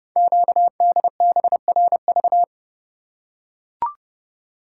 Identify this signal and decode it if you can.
Morse code, heard as a keyed tone, about 31 wpm, 690 Hz. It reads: QDBRV
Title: Code Groups - 5 Letters - 30wpm